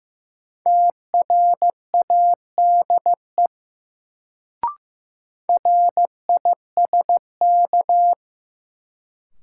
Morse code, heard as a keyed tone, about 15 wpm, 700 Hz. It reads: TRADE RISK
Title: Sentences from Top 600 Words - New Word Drill - 15wpm